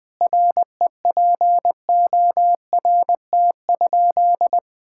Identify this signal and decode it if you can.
Morse code, heard as a keyed tone, about 21 wpm, 700 Hz. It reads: REPORT?
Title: Sentences from Top 1000 Words - Review Part 6 - 20wpm